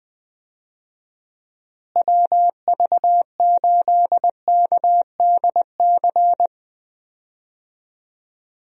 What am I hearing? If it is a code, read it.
Morse code, heard as a keyed tone, about 21 wpm, 700 Hz. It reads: WV8KDC
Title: USA Call Signs - 6 Characters - Repeated 2x - 20wpm